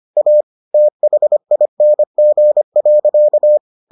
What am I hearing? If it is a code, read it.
Morse code, heard as a keyed tone, about 25 wpm, 600 Hz. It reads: A THING.